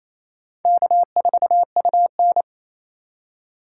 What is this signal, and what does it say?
Morse code, heard as a keyed tone, about 35 wpm, 700 Hz. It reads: K4UN